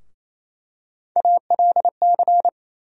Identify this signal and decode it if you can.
Morse code, heard as a keyed tone, about 28 wpm, 700 Hz. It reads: ALC
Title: USA Call Sign Postfixes - 3 Letters 28wpm